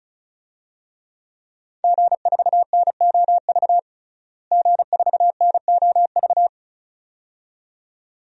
Morse code, heard as a keyed tone, about 36 wpm, 700 Hz. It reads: G4NOV G4NOV
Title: DXCC-100 Call Signs - 5 Characters - Repeated 2x - 35wpm